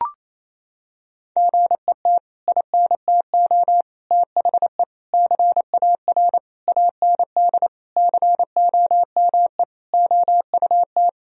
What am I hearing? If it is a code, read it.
Morse code, heard as a keyed tone, about 28 wpm, 700 Hz. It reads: GET INTO THE CAR AND COME OUT
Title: Sentences from Top 1000 Words - Review Part 1 - 28wpm